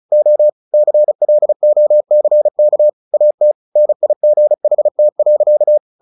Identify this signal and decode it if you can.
Morse code, heard as a keyed tone, about 34 wpm, 600 Hz. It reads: O CLOCK AT NIGHT.
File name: the old man and the sea 2 35wpm